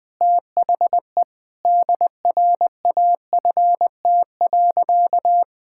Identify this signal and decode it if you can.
Morse code, heard as a keyed tone, about 21 wpm, 700 Hz. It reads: THE DRAFT.
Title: Sentences with 6-Word Phrases - 20wpm